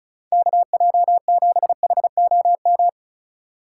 Morse code, heard as a keyed tone, about 34 wpm, 700 Hz. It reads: KJ7HOM
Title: USA Call Signs - 6 Characters - 3x Character Spacing - 35wpm